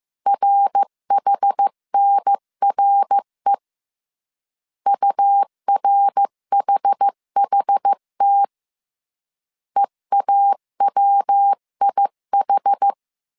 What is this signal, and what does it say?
Morse code, heard as a keyed tone, about 15 wpm, 800 Hz. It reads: RHNRE URHHT EAWIH
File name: CW / DET / 1 3 1500C S A 14wpm 2447